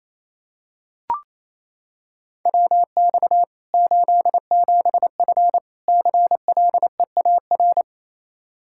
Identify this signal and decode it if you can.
Morse code, heard as a keyed tone, about 27 wpm, 700 Hz. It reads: WX 87F CLEAR
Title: Practice QSO - Exchange 2 - Info - 28wpm